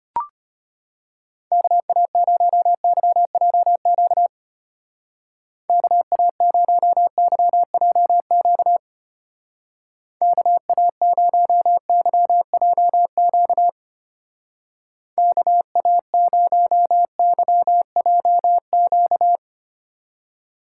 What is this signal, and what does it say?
Morse code, heard as a keyed tone, about 31 wpm, 700 Hz. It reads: KA0YJQ KA0YJQ KA0YJQ KA0YJQ
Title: Speed Racing - USA Call Signs 25wpm